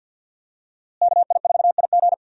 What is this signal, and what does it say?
Morse code, heard as a keyed tone, about 52 wpm, 700 Hz. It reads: KI4IG
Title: USA Call Signs - 5 Characters - Repeated 3x - 50wpm